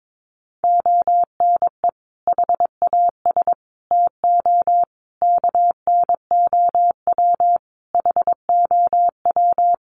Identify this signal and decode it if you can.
Morse code, heard as a keyed tone, about 22 wpm, 700 Hz. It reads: ONE HAS TO KNOW HOW
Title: Sentences from Top 800 Words 22wpm